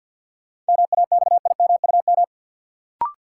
Morse code, heard as a keyed tone, about 53 wpm, 700 Hz. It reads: MAXIMUM
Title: Sentences from Top 1300 Words - New Word Drill Alt - 50wpm